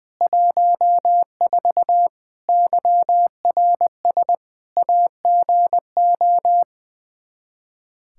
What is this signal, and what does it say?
Morse code, heard as a keyed tone, about 20 wpm, 700 Hz. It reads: 14 YRS AGO
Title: Practice QSO - Exchange 3 - More Info - 20wpm